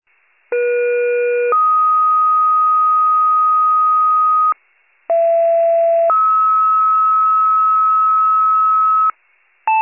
Two-tone page, heard as a one-second tone, then a three-second tone, about 480 then 1250 Hz, about 670 then 1290 Hz. It is heard from 0.5 to 9.1 s.